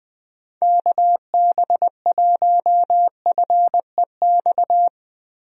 Morse code, heard as a keyed tone, about 21 wpm, 700 Hz. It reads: KB1FEX
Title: USA Call Signs - 6 Characters - Repeated 2x - 20wpm